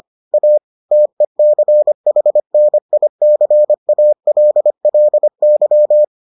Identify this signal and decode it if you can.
Morse code, heard as a keyed tone, about 25 wpm, 600 Hz. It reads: A TECHNICALLY